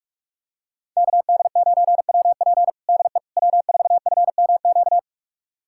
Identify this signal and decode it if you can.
Morse code, heard as a keyed tone, about 45 wpm, 700 Hz. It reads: KD9WP DE W4FMQ
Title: USA Callsign DE Callsign - 5 Characters - 45wpm